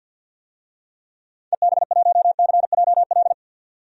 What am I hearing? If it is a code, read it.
Morse code, heard as a keyed tone, about 51 wpm, 700 Hz. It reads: EB1CJL
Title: DXCC-100 Call Signs - 6 Characters - 50wpm